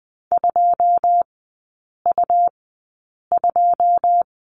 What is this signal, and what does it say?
Morse code, heard as a keyed tone, about 20 wpm, spaced as for 8 wpm, 700 Hz. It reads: 2U2
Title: 127 Lesson 21e - Number 6 - Three Characters 20wpm